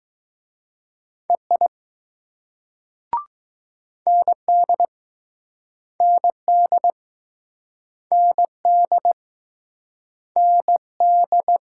Morse code, heard as a keyed tone, about 20 wpm, 700 Hz. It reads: EI ND ND ND ND